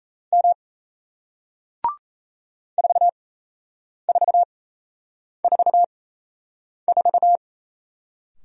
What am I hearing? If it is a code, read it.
Morse code, heard as a keyed tone, about 38 wpm, spaced as for 8 wpm, 700 Hz. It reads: M 4444